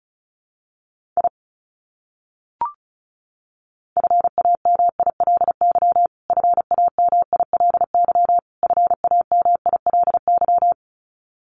Morse code, heard as a keyed tone, about 34 wpm, 700 Hz. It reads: I FAMILY FAMILY FAMILY
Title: Top 500 Words Repeated 35wpm